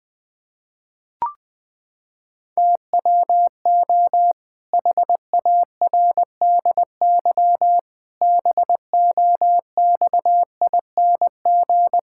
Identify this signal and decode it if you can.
Morse code, heard as a keyed tone, about 20 wpm, 700 Hz. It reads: TWO HARDY BOXING
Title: Pangram Challenge v2 - 20wpm